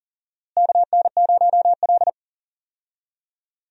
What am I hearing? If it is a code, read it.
Morse code, heard as a keyed tone, about 40 wpm, 700 Hz. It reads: KN0L
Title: USA Call Signs - 4 Characters - Repeated 3x - 40wpm